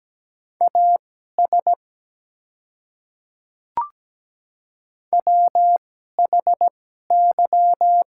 Morse code, heard as a keyed tone, about 17 wpm, spaced as for 12 wpm, 700 Hz. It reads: AS WHY
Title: 2x Character Spacing - Top 300 Words - 17wpm